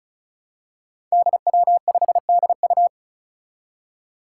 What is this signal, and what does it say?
Morse code, heard as a keyed tone, about 36 wpm, 700 Hz. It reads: DW5DU
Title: Five Challenging Letter-Number Combinations 35wpm